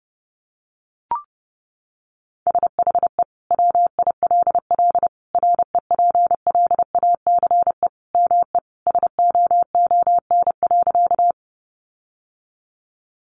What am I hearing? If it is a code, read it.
Morse code, heard as a keyed tone, about 30 wpm, 700 Hz. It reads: SHE WILL REPLACE ME SOON.
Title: Sentences from Top 1400 Words 30wpm